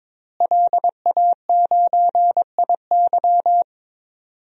Morse code, heard as a keyed tone, about 21 wpm, 700 Hz. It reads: LA9IY
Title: DXCC-100 Call Signs - 5 Characters - Repeated 2x - 22wpm